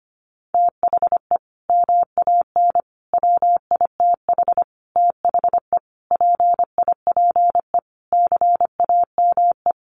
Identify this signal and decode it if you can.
Morse code, heard as a keyed tone, about 25 wpm, 700 Hz. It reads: THE MAN WITH THE PIPE CAME
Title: Sentences from Top 1500 Words 25wpm